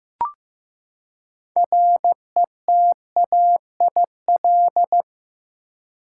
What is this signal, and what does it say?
Morse code, heard as a keyed tone, about 15 wpm, 700 Hz. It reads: RETAIL
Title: Practice QSO - Exchange 3 - More Info - 15wpm